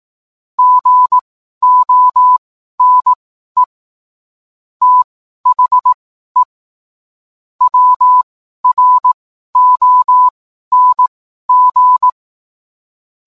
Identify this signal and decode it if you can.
Morse code, heard as a keyed tone, about 18 wpm, spaced as for 12 wpm, 1000 Hz. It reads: GONE THE WRONG